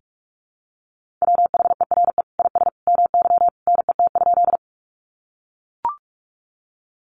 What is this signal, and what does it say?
Morse code, heard as a keyed tone, about 44 wpm, 700 Hz. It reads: WHERE IS MY NET?
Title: Sentences from Top 1400 Words 45wpm